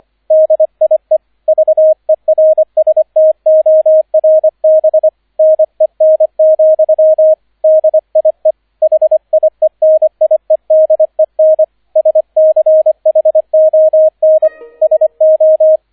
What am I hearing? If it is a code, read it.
Morse code, heard as a keyed tone, about 25 wpm, 630 Hz, with radio noise. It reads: DIE VERSTORBNEN, DIE HIENIEDEN SCHON SO